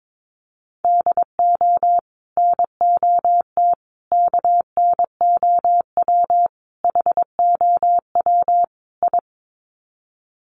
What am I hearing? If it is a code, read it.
Morse code, heard as a keyed tone, about 22 wpm, 700 Hz. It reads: DO NOT KNOW HOW I
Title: Sets of 5 Words Encore2 22wpm